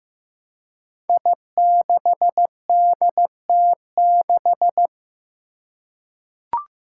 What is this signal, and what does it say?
Morse code, heard as a keyed tone, about 15 wpm, 700 Hz. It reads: I6DT6
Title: Five Challenging Letter-Number Combinations 15wpm